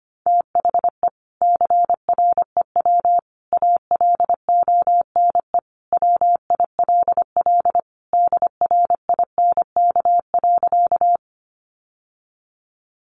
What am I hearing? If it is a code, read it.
Morse code, heard as a keyed tone, about 25 wpm, 700 Hz. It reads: THE CREW ALONE WILL DRINK.
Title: Sentences from Top 1500 Words 25wpm